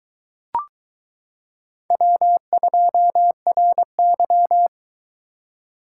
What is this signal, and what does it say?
Morse code, heard as a keyed tone, about 22 wpm, 700 Hz. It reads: W2RY